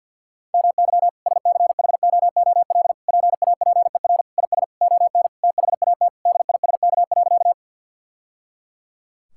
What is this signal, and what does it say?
Morse code, heard as a keyed tone, about 50 wpm, 700 Hz. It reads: MY SCHOOL PAPER IS ON THAT DISK.
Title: Sentences from Top 1000 Words - Review Part 6 - 50wpm